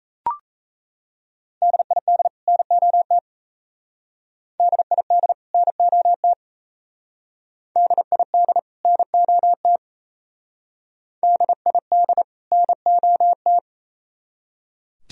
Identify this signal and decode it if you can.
Morse code, heard as a keyed tone, about 36 wpm, 700 Hz. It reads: DID NOT DID NOT DID NOT DID NOT